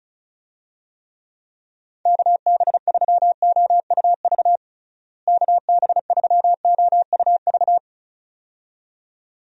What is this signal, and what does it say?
Morse code, heard as a keyed tone, about 34 wpm, 700 Hz. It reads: KB3OUV KB3OUV
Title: USA Call Signs - 6 Characters - Repeated 2x - 35wpm